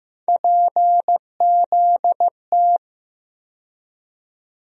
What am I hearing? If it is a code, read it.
Morse code, heard as a keyed tone, about 15 wpm, 700 Hz. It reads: PZT
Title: USA Call Sign Postfixes - 3 Letters 15wpm